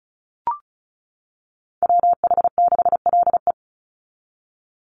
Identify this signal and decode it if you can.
Morse code, heard as a keyed tone, about 34 wpm, 700 Hz. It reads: WH6LE
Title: NA SOTA Activator Callsigns 35wpm